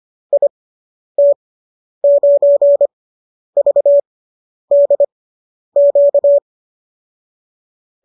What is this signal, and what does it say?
Morse code, heard as a keyed tone, about 25 wpm, spaced as for 10 wpm, 590 Hz. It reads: IT9VDQ